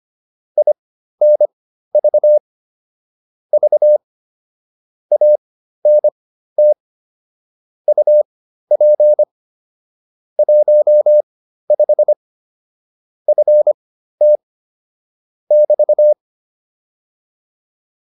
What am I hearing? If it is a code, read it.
Morse code, heard as a keyed tone, about 25 wpm, spaced as for 13 wpm, 600 Hz. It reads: INV V ANT UP 15 FT <BT>